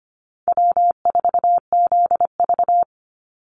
Morse code, heard as a keyed tone, about 25 wpm, 700 Hz. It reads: W4ZV
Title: NA SOTA Chaser Callsigns 25wpm